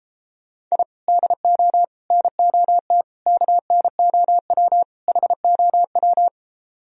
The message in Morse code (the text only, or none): I DO NOT KNOW HOW